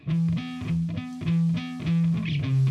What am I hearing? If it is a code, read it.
There is no code.